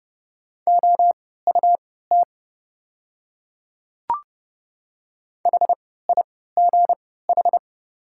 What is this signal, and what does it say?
Morse code, heard as a keyed tone, about 29 wpm, spaced as for 17 wpm, 690 Hz. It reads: OUT HIGH
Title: 3x Character Spacing - Top 300 Words - 30wpm